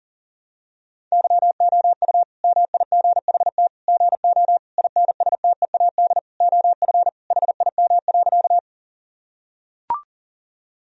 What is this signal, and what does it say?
Morse code, heard as a keyed tone, about 40 wpm, 700 Hz. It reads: YOU MIGHT GO INSTEAD OF HIM.
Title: Sentences from Top 1000 Words - Review Part 4 - 40wpm